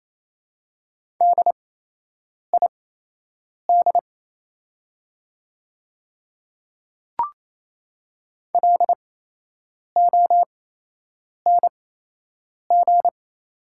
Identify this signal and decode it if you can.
Morse code, heard as a keyed tone, about 28 wpm, spaced as for 8 wpm, 700 Hz. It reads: DID LONG
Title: 8x Character Spacing - Top 100 Words - 28wpm